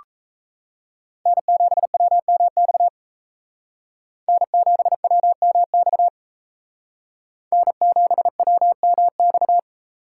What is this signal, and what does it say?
Morse code, heard as a keyed tone, about 35 wpm, 700 Hz. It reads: N7WMX N7WMX N7WMX